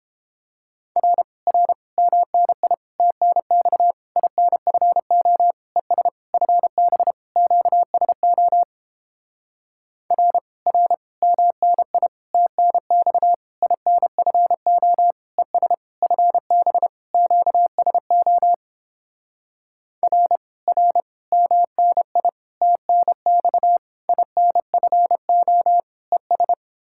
Morse code, heard as a keyed tone, about 29 wpm, 700 Hz. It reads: R R MNI TNX INFO ES FB QSO R R MNI TNX INFO ES FB QSO R R MNI TNX INFO ES